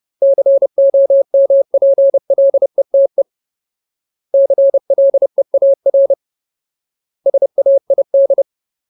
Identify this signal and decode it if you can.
Morse code, heard as a keyed tone, about 30 wpm, 550 Hz. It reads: COMPLETE CLEAR SAID
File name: Ultimate Word List / 30WPM Ultimate